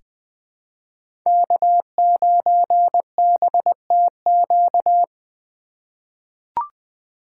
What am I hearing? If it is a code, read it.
Morse code, heard as a keyed tone, about 20 wpm, 700 Hz. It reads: K9BTQ
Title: K1USN SST QSO Practice - 20wpm